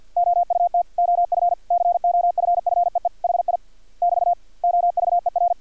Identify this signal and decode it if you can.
Morse code, heard as a keyed tone, about 50 wpm, 700 Hz. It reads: OUT OF COFFEE HI <BT> OVER
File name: archived / CRSnet